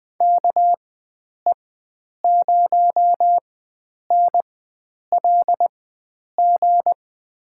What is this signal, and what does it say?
Morse code, heard as a keyed tone, about 20 wpm, spaced as for 9 wpm, 700 Hz. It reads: KE0NLG
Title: USA Call Signs - 6 Characters - 4x Character Spacing - 20wpm